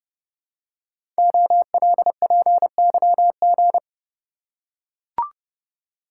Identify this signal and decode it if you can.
Morse code, heard as a keyed tone, about 28 wpm, 700 Hz. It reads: OLPYG
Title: Code Groups - 5 Letters - 30wpm